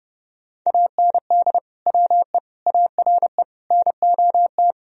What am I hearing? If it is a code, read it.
Morse code, heard as a keyed tone, about 32 wpm, 700 Hz. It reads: AND WE ARE NOT
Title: Sentences from Top 1000 Words - Review Part 1 - 30wpm